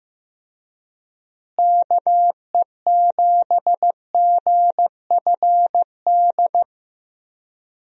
Morse code, heard as a keyed tone, about 15 wpm, 700 Hz. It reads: KE7GFD
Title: USA Call Signs - 6 Characters - Repeated 3x - 15wpm